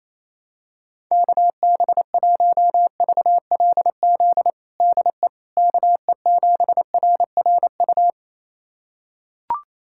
Morse code, heard as a keyed tone, about 27 wpm, 700 Hz. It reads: KB1VLZ DE KE7RRU
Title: USA Callsign DE Callsign 28wpm